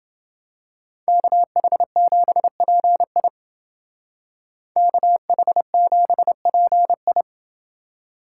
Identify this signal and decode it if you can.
Morse code, heard as a keyed tone, about 29 wpm, 700 Hz. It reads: KH7PI KH7PI